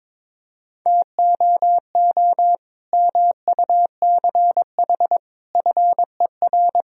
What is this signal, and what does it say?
Morse code, heard as a keyed tone, about 22 wpm, 700 Hz. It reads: TOO MUCH FER
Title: Sentences from Top 200 Words and Abbreviations 22wpm